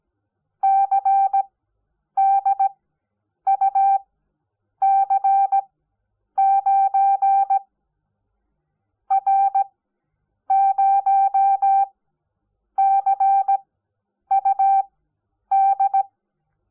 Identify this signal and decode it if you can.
Morse code, heard as a keyed tone, about 17 wpm, spaced as for 8 wpm, 780 Hz. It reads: CDUC9 R0CUD
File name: Onlinelehrgang / morsen / DJ4UF Morsen05